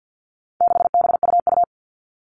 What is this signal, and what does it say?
Morse code, heard as a keyed tone, about 50 wpm, 700 Hz. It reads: 6BUU
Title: All Cut Numbers - 4-Digit 50wpm